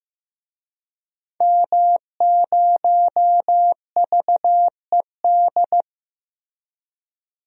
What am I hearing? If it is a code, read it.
Morse code, heard as a keyed tone, about 15 wpm, 700 Hz. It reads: M0VED
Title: DXCC-100 Call Signs - 5 Characters - 15wpm